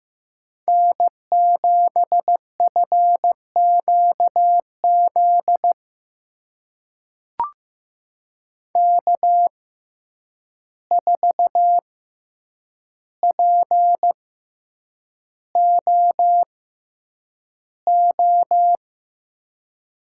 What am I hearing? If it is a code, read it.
Morse code, heard as a keyed tone, about 15 wpm, spaced as for 8 wpm, 700 Hz. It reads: N7FQZ K4POO
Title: USA Call Signs - 5 Characters - 6x Character Spacing - 15wpm